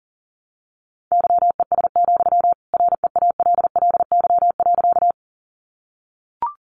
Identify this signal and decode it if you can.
Morse code, heard as a keyed tone, about 40 wpm, 700 Hz. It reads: YES, REALLY.
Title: Sentences from Top 600 Words 40wpm